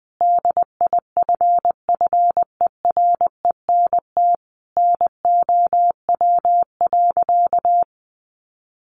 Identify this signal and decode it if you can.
Morse code, heard as a keyed tone, about 20 wpm, 700 Hz. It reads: DIFFERENT NOW.
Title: Sentences from Top 1200 Words 20wpm